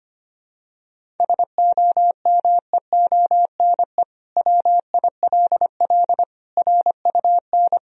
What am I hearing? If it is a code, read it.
Morse code, heard as a keyed tone, about 25 wpm, 700 Hz. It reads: SOMEONE WILL RUN
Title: Sentences from Top 1000 Words - Review Part 1 - 25wpm